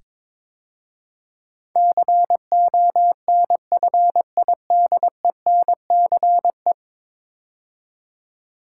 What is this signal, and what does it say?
Morse code, heard as a keyed tone, about 22 wpm, 700 Hz. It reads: CONFIDENCE
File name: practice / 22wpm / 10 Letter Words - First 3 Letters Given - 22wpm